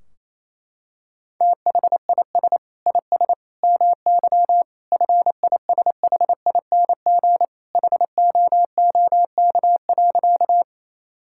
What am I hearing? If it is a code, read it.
Morse code, heard as a keyed tone, about 28 wpm, 700 Hz. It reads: THIS IS MY FISHING HOOK.
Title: Sentences from Top 1600 Words 28wpm